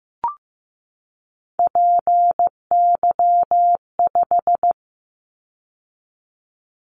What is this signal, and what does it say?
Morse code, heard as a keyed tone, about 15 wpm, 700 Hz. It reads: PY5
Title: Callsign Prefixes for Top 50 DXCC - 15wpm